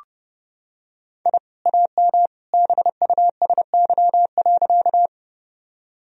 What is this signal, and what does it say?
Morse code, heard as a keyed tone, about 30 wpm, 700 Hz. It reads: I AM BUSY.
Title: Sentences from Top 1000 Words - Review Part 3 - 30wpm